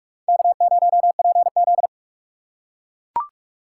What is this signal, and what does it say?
Morse code, heard as a keyed tone, about 44 wpm, 700 Hz. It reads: K0PZ